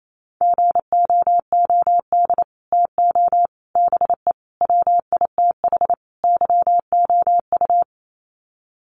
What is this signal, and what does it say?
Morse code, heard as a keyed tone, about 28 wpm, 700 Hz. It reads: GOOD TO BE WITH YOU
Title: Sets of 5 Words Encore2 28wpm